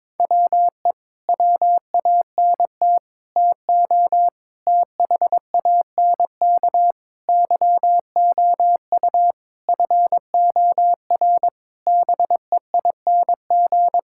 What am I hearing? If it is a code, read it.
Morse code, heard as a keyed tone, about 22 wpm, 700 Hz. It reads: WE WANT TO THANK YOU FOR BEING
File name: Sets of 9 Words - 22wpm